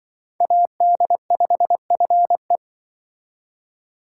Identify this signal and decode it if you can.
Morse code, heard as a keyed tone, about 25 wpm, 700 Hz. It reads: AD5FE
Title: Speed Racing - USA 2x2 Call Signs 20wpm